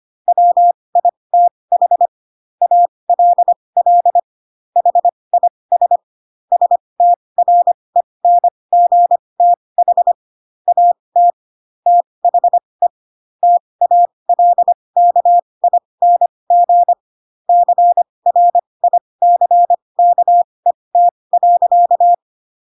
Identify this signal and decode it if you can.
Morse code, heard as a keyed tone, about 25 wpm, spaced as for 20 wpm, 700 Hz. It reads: WITH ALL HIS STRENGTH AT THE TALKING CRICKET.
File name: Pinocchio / Pinocchio ch4 15 25 ik1hge